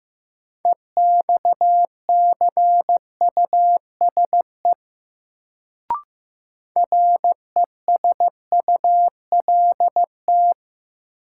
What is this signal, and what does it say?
Morse code, heard as a keyed tone, about 16 wpm, 700 Hz. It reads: EXCUSE RESULT
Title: Sentences from Top 700 Words - New Word Drill - 15wpm